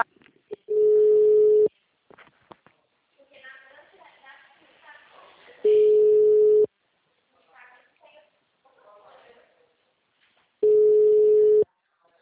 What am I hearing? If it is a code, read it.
Ringback tone (Europe, 425 Hz), with speech.